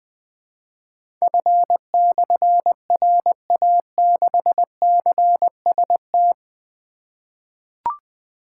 Morse code, heard as a keyed tone, about 20 wpm, 700 Hz. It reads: F/RA6CST
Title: Difficult Call Signs 20wpm